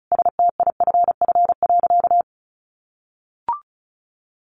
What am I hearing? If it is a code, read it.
Morse code, heard as a keyed tone, about 34 wpm, 700 Hz. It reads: STIFF.
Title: A Sound Of Thunder 35wpm